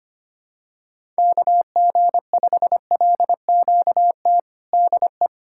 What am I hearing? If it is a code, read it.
Morse code, heard as a keyed tone, about 25 wpm, 700 Hz. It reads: KG5LQT DE
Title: USA Callsign DE Callsign - 6 Characters - 25wpm